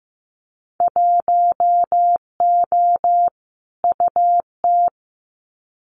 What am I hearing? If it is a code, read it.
Morse code, heard as a keyed tone, about 15 wpm, 700 Hz. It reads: 1O UT
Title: Winter Field Day Exchanges Encore 15wpm